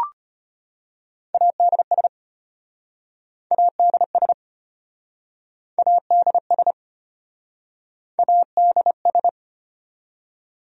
Morse code, heard as a keyed tone, about 33 wpm, 700 Hz. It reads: ADS ADS ADS ADS